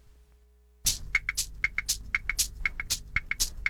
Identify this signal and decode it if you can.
Emergency Alert System (SAME) tones, heard as harsh buzzing bursts. It is heard from 1.0 to 3.4 s.